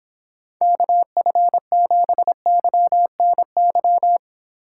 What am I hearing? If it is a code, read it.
Morse code, heard as a keyed tone, about 28 wpm, 700 Hz. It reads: KF7YNY